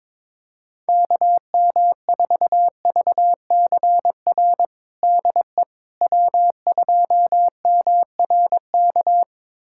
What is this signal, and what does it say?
Morse code, heard as a keyed tone, about 23 wpm, 700 Hz. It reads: KM4VCR DE W2MRK
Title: USA Callsign DE Callsign 22wpm